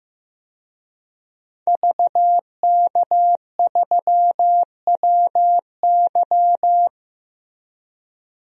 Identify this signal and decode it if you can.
Morse code, heard as a keyed tone, about 15 wpm, 700 Hz. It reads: VK3WY
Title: DXCC-100 Call Signs - 5 Characters - Repeated 2x - 15wpm